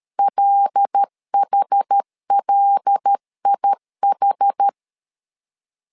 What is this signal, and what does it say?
Morse code, heard as a keyed tone, about 13 wpm, 800 Hz. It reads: LHLIH
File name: LEZIONI / 5 LBVC / 5 L 12wpm 1913